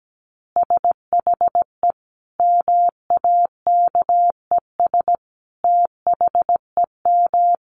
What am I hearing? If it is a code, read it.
Morse code, heard as a keyed tone, about 17 wpm, 700 Hz. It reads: SHE MAKES THEM
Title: Easy Phrases 17wpm